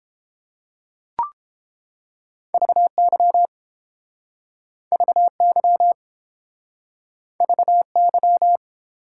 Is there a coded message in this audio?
Yes: Morse code from 2.5 to 8.6 s, heard as a keyed tone, about 30 wpm, 700 Hz. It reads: VY VY VY